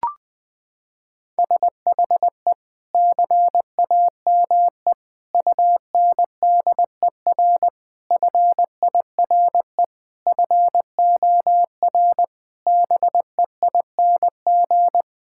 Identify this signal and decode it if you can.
Morse code, heard as a keyed tone, about 21 wpm, 700 Hz. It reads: SHE CAME UNDER FIRE FOR BEING
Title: Sentences from Top 1000 Words - Review Part 6 - 20wpm